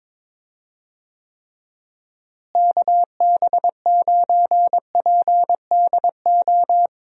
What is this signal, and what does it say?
Morse code, heard as a keyed tone, about 22 wpm, 700 Hz. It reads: KB9PDO
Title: USA Call Signs - 6 Characters - Repeated 3x - 22wpm